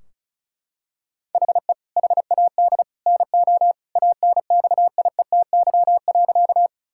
Morse code, heard as a keyed tone, about 34 wpm, 700 Hz. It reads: HE HAD NO ANXIETY.
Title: Sentences from Top 1600 Words 35wpm